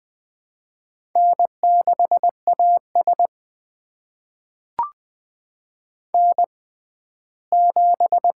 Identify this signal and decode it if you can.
Morse code, heard as a keyed tone, about 20 wpm, spaced as for 14 wpm, 700 Hz. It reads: N6AS N7
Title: USA Call Signs - 4 Characters - 6x Character Spacing - 20wpm